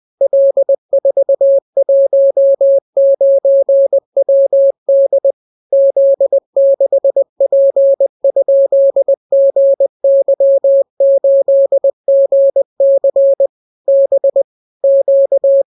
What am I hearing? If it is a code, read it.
Morse code, heard as a keyed tone, about 20 wpm, 550 Hz. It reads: L419WD Z6P?GY8GC B Q